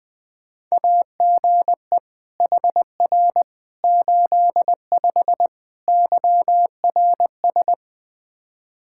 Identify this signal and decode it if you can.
Morse code, heard as a keyed tone, about 21 wpm, 700 Hz. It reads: AGE HR 85 YRS
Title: Practice QSO - Exchange 3 - More Info - 20wpm